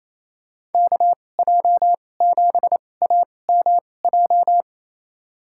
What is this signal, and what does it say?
Morse code, heard as a keyed tone, about 28 wpm, spaced as for 20 wpm, 700 Hz. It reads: KJ7AMJ
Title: USA Call Signs - 6 Characters - 2x Character Spacing - 28wpm